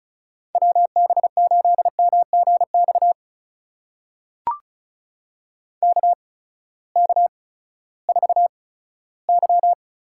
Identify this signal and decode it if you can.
Morse code, heard as a keyed tone, about 36 wpm, spaced as for 18 wpm, 700 Hz. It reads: WB8MGX KK4Y
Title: USA Call Signs - 6 Characters - 8x Character Spacing - 35wpm